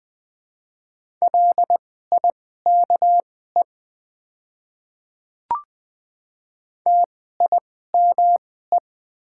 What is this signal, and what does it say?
Morse code, heard as a keyed tone, about 20 wpm, spaced as for 14 wpm, 700 Hz. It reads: LIKE TIME
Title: 2x Character Spacing - Top 100 Words - 20wpm